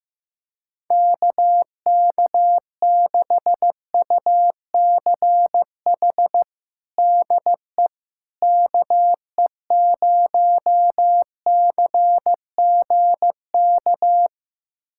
Morse code, heard as a keyed tone, about 15 wpm, 700 Hz. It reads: KK6UCH DE KE0CGK
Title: USA Callsign DE Callsign 15wpm